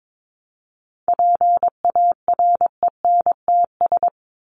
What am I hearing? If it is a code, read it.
Morse code, heard as a keyed tone, about 22 wpm, 700 Hz. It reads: PARENTS
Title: ES:S Suffix - Top 100 Words with Base Words 22wpm